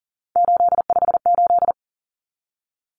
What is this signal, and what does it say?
Morse code, heard as a keyed tone, about 40 wpm, 700 Hz. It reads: 858
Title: Numbers - 3-Digit 40wpm